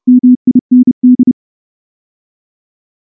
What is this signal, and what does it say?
Morse code, heard as a keyed tone, about 28 wpm, 250 Hz. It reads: MIND
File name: morse / morse-training-30WPM